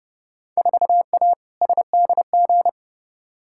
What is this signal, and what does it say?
Morse code, heard as a keyed tone, about 30 wpm, 700 Hz. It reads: 4A SDG
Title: Field Day Exchanges - 30wpm